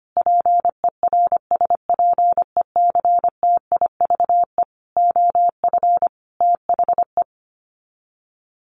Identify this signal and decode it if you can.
Morse code, heard as a keyed tone, about 25 wpm, 700 Hz. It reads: PERSPECTIVE OF THE
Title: Sets of 5 Words Encore2 25wpm